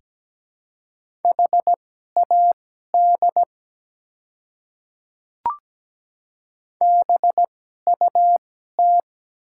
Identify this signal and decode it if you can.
Morse code, heard as a keyed tone, about 17 wpm, spaced as for 12 wpm, 700 Hz. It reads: HAD BUT
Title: 2x Character Spacing - Top 100 Words - 17wpm